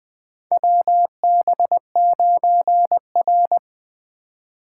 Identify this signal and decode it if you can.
Morse code, heard as a keyed tone, about 20 wpm, 700 Hz. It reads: WB9R
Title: USA Call Signs - 3x Character Spacing - 20wpm